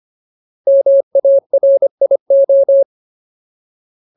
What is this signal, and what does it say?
Morse code, heard as a keyed tone, about 25 wpm, 550 Hz. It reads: MARIO
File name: Ultimate Word List / 25WPM Ultimate